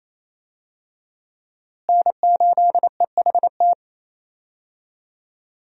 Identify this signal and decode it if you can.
Morse code, heard as a keyed tone, about 28 wpm, 700 Hz. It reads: N8EHT
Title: USA Call Signs - 5 Characters - Repeated 3x - 28wpm